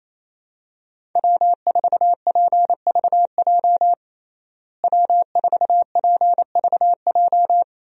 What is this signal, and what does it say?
Morse code, heard as a keyed tone, about 28 wpm, 700 Hz. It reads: W4PVJ W4PVJ
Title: USA Call Signs - 5 Characters - Repeated 2x - 28wpm